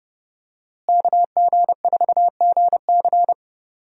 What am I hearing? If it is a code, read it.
Morse code, heard as a keyed tone, about 30 wpm, 700 Hz. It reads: KG4GC
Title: USA Call Signs - 5 Characters - Repeated 3x - 30wpm